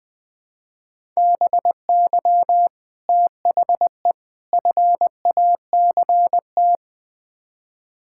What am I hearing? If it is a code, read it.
Morse code, heard as a keyed tone, about 20 wpm, 700 Hz. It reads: BY THE FACT
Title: Build A Phrase - 4-Word Phrases - 20wpm